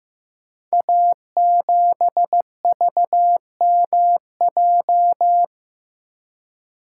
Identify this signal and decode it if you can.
Morse code, heard as a keyed tone, about 15 wpm, 700 Hz. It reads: A7VMJ
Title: Code Groups - 5 Letter:Numbers - 15wpm